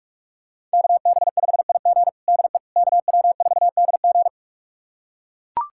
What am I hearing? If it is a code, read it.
Morse code, heard as a keyed tone, about 42 wpm, 700 Hz. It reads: KB5IG DE KW4DG
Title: USA Callsign DE Callsign - 5 Characters - 45wpm